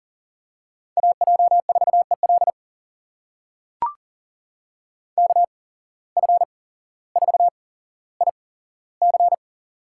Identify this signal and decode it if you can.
Morse code, heard as a keyed tone, about 40 wpm, spaced as for 17 wpm, 700 Hz. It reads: AJ4EL KF4IC
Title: USA Call Signs - 5 Characters - 8x Character Spacing - 40wpm